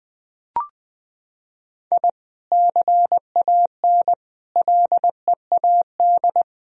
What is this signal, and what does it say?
Morse code, heard as a keyed tone, about 20 wpm, 700 Hz. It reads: I CAN LEAD
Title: Sentences from Top 1000 Words - Review Part 2 - 20wpm